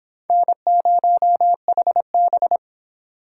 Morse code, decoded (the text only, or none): N0HB